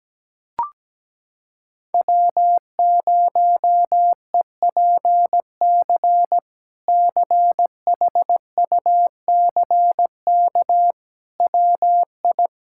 Morse code, heard as a keyed tone, about 17 wpm, 700 Hz. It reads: W0EPC CHUCK WI
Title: K1USN SST QSO Practice - 17wpm